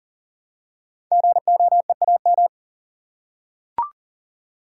Morse code, heard as a keyed tone, about 40 wpm, 700 Hz. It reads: GOEAM
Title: Code Groups - 5 Letters - 40wpm